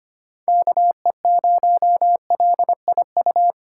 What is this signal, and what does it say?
Morse code, heard as a keyed tone, about 25 wpm, 700 Hz. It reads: KE0LIU
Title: USA Call Signs - 8x Character Spacing - 25wpm